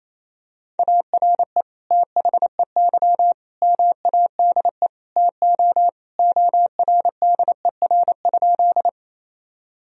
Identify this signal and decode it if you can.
Morse code, heard as a keyed tone, about 27 wpm, 700 Hz. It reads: ARE THEY MADE TO ORDER?
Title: Sentences from Top 1000 Words - Review Part 2 - 28wpm